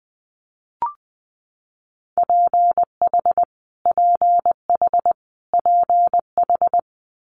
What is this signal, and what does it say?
Morse code, heard as a keyed tone, about 20 wpm, 700 Hz. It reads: PH PH PH
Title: Multi-letter Phonemes Repeated 3x 20wpm